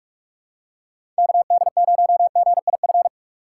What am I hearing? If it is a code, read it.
Morse code, heard as a keyed tone, about 45 wpm, 700 Hz. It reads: KD0GIF
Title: USA Call Signs - 6 Characters - 3x Character Spacing - 45wpm